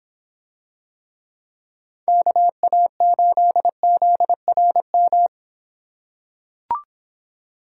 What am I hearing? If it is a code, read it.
Morse code, heard as a keyed tone, about 26 wpm, 700 Hz. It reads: KA8ZRM